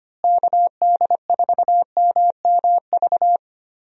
Morse code, heard as a keyed tone, about 25 wpm, 700 Hz. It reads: KD4MMV
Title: USA Call Signs - 6 Characters - Repeated 3x - 25wpm